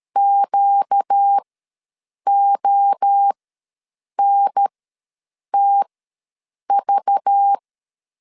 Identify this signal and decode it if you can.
Morse code, heard as a keyed tone, about 13 wpm, spaced as for 7 wpm, 800 Hz. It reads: QONTV
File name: LEZIONI / 6 PKX0 / 6 PKX ws 12wpm 3106